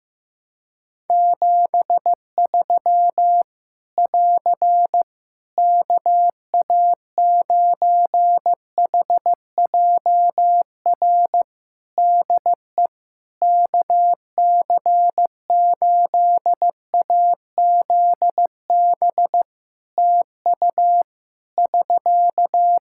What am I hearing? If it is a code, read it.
Morse code, heard as a keyed tone, about 15 wpm, 700 Hz. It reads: 73 <AR> KA9HJR DE KC8AZB TU <SK>
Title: Practice QSO - Exchange 4 - Ending - 15wpm